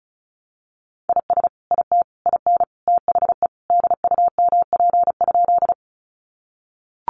Morse code, heard as a keyed tone, about 34 wpm, 700 Hz. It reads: IS IT IN THE DUMP?
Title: Sentences from Top 1300 Words 35wpm